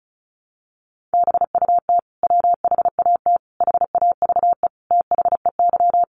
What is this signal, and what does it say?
Morse code, heard as a keyed tone, about 34 wpm, 700 Hz. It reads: BUT WHAT HAVE THEY
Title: Sentences from Top 100 Words 35wpm